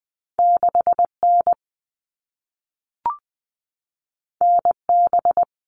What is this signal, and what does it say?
Morse code, heard as a keyed tone, about 20 wpm, 700 Hz. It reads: 6N NB
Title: All Cut Numbers - 2-Digit 20wpm